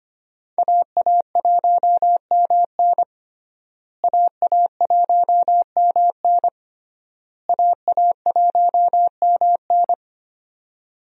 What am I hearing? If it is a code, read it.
Morse code, heard as a keyed tone, about 25 wpm, 700 Hz. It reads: AA1MN AA1MN AA1MN
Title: USA Call Signs - 5 Characters - Repeated 3x - 25wpm